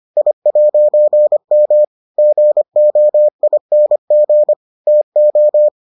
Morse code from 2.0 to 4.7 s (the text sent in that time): GOING